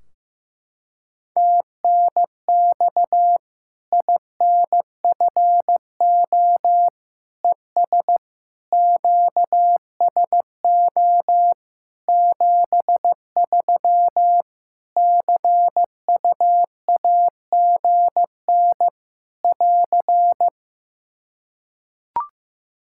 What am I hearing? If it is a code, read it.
Morse code, heard as a keyed tone, about 16 wpm, 700 Hz. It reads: TNX INFO ES QSO 73 CUAGN <AR>
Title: Practice QSO - Exchange 4 - Ending - 15wpm